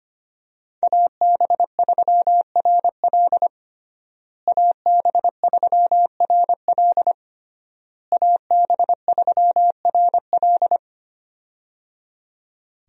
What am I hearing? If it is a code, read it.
Morse code, heard as a keyed tone, about 25 wpm, 700 Hz. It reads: AB3RL AB3RL AB3RL
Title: USA Call Signs - 5 Characters - Repeated 3x - 25wpm